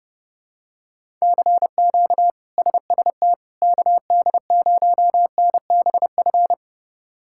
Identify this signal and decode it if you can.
Morse code, heard as a keyed tone, about 32 wpm, 700 Hz. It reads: CQ SST KD0NBF
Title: K1USN SST QSO Practice - 30wpm